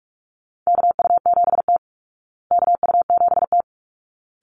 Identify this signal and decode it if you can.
Morse code, heard as a keyed tone, about 44 wpm, 700 Hz. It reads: KU7T KU7T
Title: Call Signs Repeated 3x - 45wpm